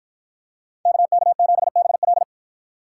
Morse code, heard as a keyed tone, about 53 wpm, 700 Hz. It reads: KK7BL